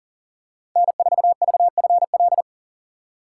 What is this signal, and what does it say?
Morse code, heard as a keyed tone, about 40 wpm, 700 Hz. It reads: N4VFL
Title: USA Call Signs - 3x Character Spacing - 40wpm